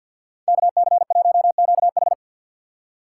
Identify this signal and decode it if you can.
Morse code, heard as a keyed tone, about 60 wpm, 700 Hz. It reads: KC1QH
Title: Speed Racing - USA Call Signs 50wpm